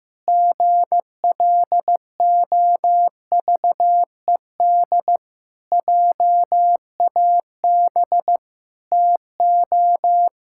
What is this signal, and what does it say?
Morse code, heard as a keyed tone, about 15 wpm, 700 Hz. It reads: GLOVED JAB TO
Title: Pangram Challenge v2 - 15wpm